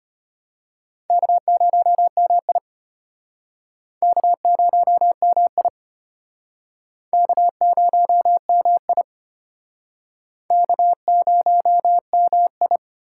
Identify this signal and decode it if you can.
Morse code, heard as a keyed tone, about 34 wpm, 700 Hz. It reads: K0MI K0MI K0MI K0MI